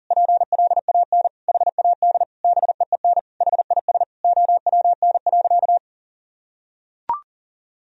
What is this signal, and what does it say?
Morse code, heard as a keyed tone, about 37 wpm, 700 Hz. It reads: PLAN HAD BEEN HIS OWN.
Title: Sentences from Top 1000 Words - Review Part 2 - 40wpm